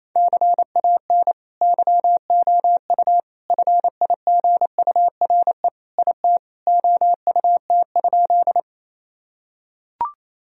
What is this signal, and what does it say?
Morse code, heard as a keyed tone, about 27 wpm, 700 Hz. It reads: CAN YOU FIGURE IT OUT?
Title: Sentences from Top 1000 Words - Review Part 2 - 28wpm